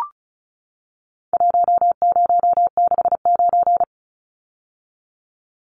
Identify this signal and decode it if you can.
Morse code, heard as a keyed tone, about 34 wpm, 700 Hz. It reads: 1069
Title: Numbers - Contest Style - 4-Digit 35wpm